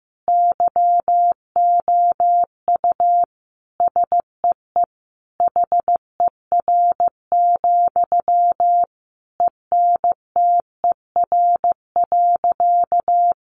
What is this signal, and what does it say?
Morse code, heard as a keyed tone, about 15 wpm, 700 Hz. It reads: YOU SEE HER, ENTER.
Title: Sentences from Top 1300 Words 15wpm